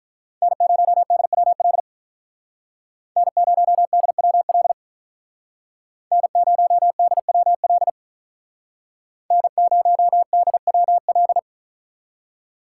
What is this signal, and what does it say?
Morse code, heard as a keyed tone, about 45 wpm, 700 Hz. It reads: N0DWL N0DWL N0DWL N0DWL